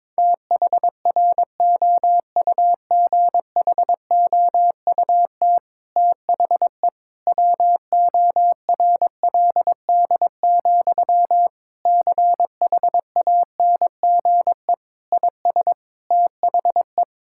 Morse code, heard as a keyed tone, about 22 wpm, 700 Hz. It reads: THROUGHOUT THE WORLD, CHANGE IS THE
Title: Sentences from Top 1000 Words - Review Part 5 - 22wpm